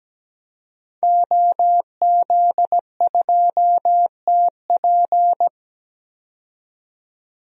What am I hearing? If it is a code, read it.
Morse code, heard as a keyed tone, about 17 wpm, 700 Hz. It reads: OZ2TP
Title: DXCC-100 Call Signs - 5 Characters - 17wpm